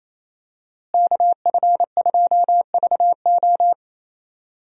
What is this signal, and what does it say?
Morse code, heard as a keyed tone, about 28 wpm, 700 Hz. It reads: KF2VO
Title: USA Call Signs - 5 Characters - Repeated 3x - 28wpm